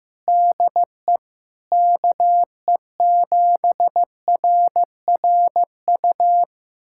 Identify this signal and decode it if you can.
Morse code, heard as a keyed tone, about 15 wpm, 700 Hz. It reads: DE KE7RRU
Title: USA Callsign DE Callsign 15wpm